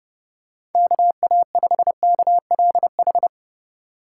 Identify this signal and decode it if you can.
Morse code, heard as a keyed tone, about 32 wpm, 700 Hz. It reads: KA5KLH
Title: USA Call Signs - 6 Characters - 2x Character Spacing - 30wpm